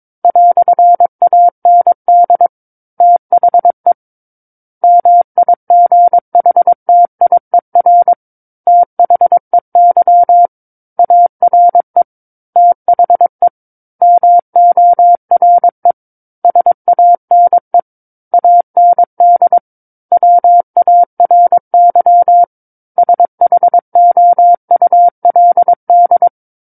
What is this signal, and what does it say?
Morse code, heard as a keyed tone, about 22 wpm, 700 Hz. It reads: "AND THE MIGHTIER THEY ARE THE MORE SANE AND WARY SHOULD
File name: Book One / 13-Chapter 13 - 22 WPM